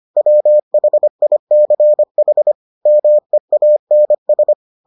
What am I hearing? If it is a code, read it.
Morse code, heard as a keyed tone, about 25 wpm, 600 Hz. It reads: WHICH MEANS